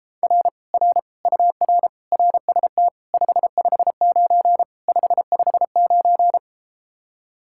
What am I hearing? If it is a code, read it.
Morse code, heard as a keyed tone, about 33 wpm, 700 Hz. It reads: R R UR RST 559 559